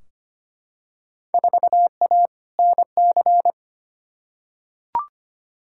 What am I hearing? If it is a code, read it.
Morse code, heard as a keyed tone, about 25 wpm, 700 Hz. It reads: 4A NC
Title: Field Day Exchanges Encore - 25wpm